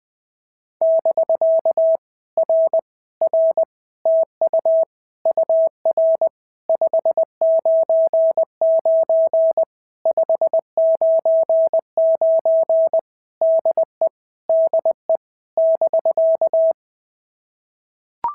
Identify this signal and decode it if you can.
Morse code, heard as a keyed tone, about 19 wpm, 650 Hz. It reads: <BK> R R TU UR 599 599 DE DE <BK>
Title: POTA QSO Practice - 20wpm